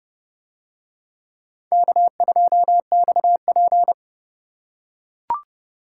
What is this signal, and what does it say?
Morse code, heard as a keyed tone, about 30 wpm, 700 Hz. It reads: K2XP